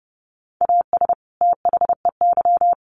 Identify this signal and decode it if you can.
Morse code, heard as a keyed tone, about 30 wpm, 700 Hz. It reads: AS THEY
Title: Sets of 2 Words 30wpm